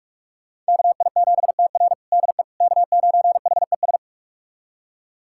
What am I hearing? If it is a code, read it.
Morse code, heard as a keyed tone, about 46 wpm, 700 Hz. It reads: KI7TR DE K9HES
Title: USA Callsign DE Callsign - 5 Characters - 45wpm